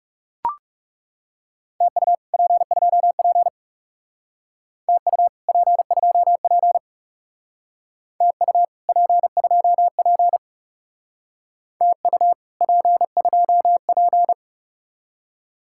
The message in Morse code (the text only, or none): TU P2P TU P2P TU P2P TU P2P